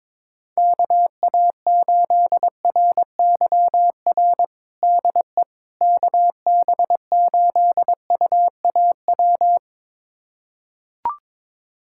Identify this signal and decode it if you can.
Morse code, heard as a keyed tone, about 22 wpm, 700 Hz. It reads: KA8RYR DE KB8UAW
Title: USA Callsign DE Callsign 22wpm